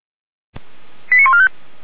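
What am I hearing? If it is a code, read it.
Five-tone selective call (ZVEI), heard as quick runs of five tones, tones of about 80 ms. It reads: 89145
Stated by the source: ZVEI five-tone selective call received on SDR.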